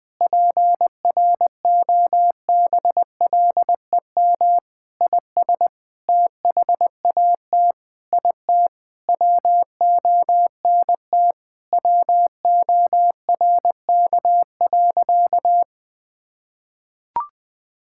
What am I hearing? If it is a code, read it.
Morse code, heard as a keyed tone, about 20 wpm, 700 Hz. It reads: PROBLEM IS THAT IT WONT WORK.
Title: Sentences with 3-Word Phrases - 20wpm